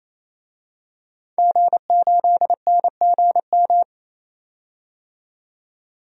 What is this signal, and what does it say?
Morse code, heard as a keyed tone, about 28 wpm, 700 Hz. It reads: G8NGM
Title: DXCC-100 Call Signs - 5 Characters - 28wpm